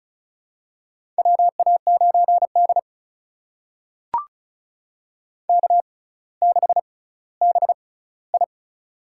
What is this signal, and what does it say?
Morse code, heard as a keyed tone, about 36 wpm, spaced as for 18 wpm, 700 Hz. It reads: WA9D K6BI
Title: USA Call Signs - 4 Characters - 6x Character Spacing - 35wpm